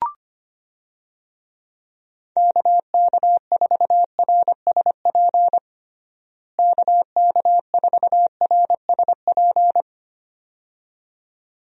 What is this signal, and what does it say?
Morse code, heard as a keyed tone, about 25 wpm, 700 Hz. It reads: KK4RSP KK4RSP
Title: USA Call Signs - 6 Characters - Repeated 2x - 25wpm